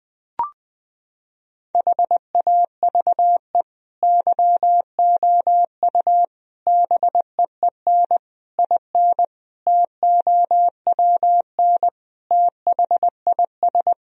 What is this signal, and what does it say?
Morse code, heard as a keyed tone, about 21 wpm, 700 Hz. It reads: HAVE YOU BEEN IN TOWN THIS
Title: Sentences from Top 1000 Words - Review Part 2 - 20wpm